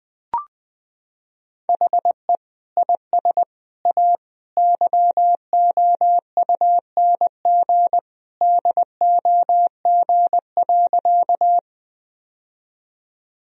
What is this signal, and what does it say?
Morse code, heard as a keyed tone, about 20 wpm, 700 Hz. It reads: HE IS A YOUNG DOG.
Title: Sentences from Top 1000 Words - Review Part 2 - 20wpm